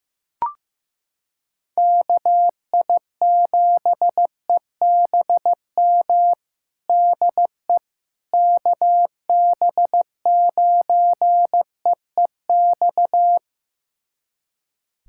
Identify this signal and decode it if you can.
Morse code, heard as a keyed tone, about 16 wpm, 700 Hz. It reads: KI7EBM DE KB9EEX
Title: USA Callsign DE Callsign - 6 Characters - 15wpm